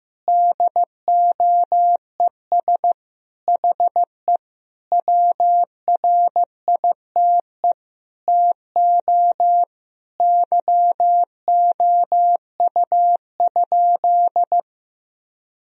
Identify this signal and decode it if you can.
Morse code, heard as a keyed tone, about 15 wpm, 700 Hz. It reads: DOES HE WRITE TO YOU?
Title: Sentences from Top 1000 Words - Review Part 1 - 15wpm